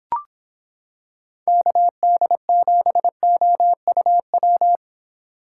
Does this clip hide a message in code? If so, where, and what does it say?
Yes: Morse code from 1.5 to 4.8 s, heard as a keyed tone, about 25 wpm, 700 Hz. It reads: KD7OUW